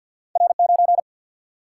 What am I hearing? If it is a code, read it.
Morse code, heard as a keyed tone, about 50 wpm, 700 Hz. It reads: R 9
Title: Single-Letter Single-Number 50wpm